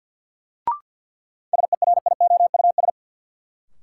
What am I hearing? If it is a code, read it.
Morse code, heard as a keyed tone, about 46 wpm, 700 Hz. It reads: SERIOUS
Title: Sentences from Top 1100 Words - New Word Drill - 50wpm